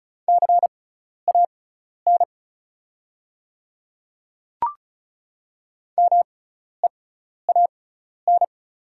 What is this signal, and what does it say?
Morse code, heard as a keyed tone, about 36 wpm, spaced as for 12 wpm, 700 Hz. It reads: CAN MEAN
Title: 6x Character Spacing - Top 300 Words - 35wpm